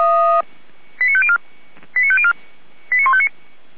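Five-tone selective call (ZVEI), heard as quick runs of five tones, tones of about 70 ms. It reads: 89504 89504 89159
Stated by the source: Five-tone ZVEI selective call received on SDR.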